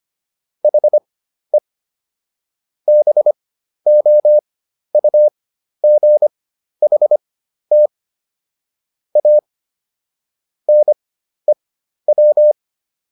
Morse code, heard as a keyed tone, about 25 wpm, spaced as for 12 wpm, 600 Hz. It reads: HE BOUGHT A NEW